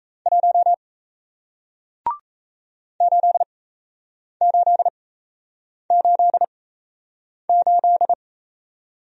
Morse code, heard as a keyed tone, about 39 wpm, spaced as for 8 wpm, 700 Hz. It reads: 1 8888